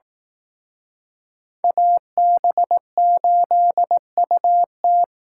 Morse code, heard as a keyed tone, about 18 wpm, 700 Hz. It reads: AB8UT
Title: Speed Racing - USA Call Signs 15wpm